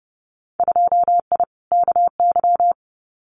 Morse code, heard as a keyed tone, about 30 wpm, 700 Hz. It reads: 2I KY
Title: Winter Field Day Exchanges 30wpm